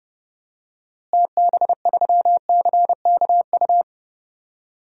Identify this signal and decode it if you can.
Morse code, heard as a keyed tone, about 30 wpm, 700 Hz. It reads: TB3CKU
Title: DXCC-100 Call Signs - 6 Characters - Repeated 2x - 30wpm